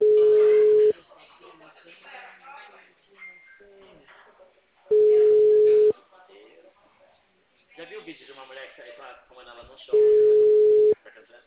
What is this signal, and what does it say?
Ringback tone (Europe, 425 Hz), with speech.